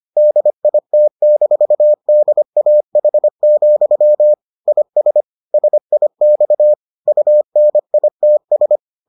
Morse code, heard as a keyed tone, about 25 wpm, 600 Hz. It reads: DIT-DAH, IS SIX UNITS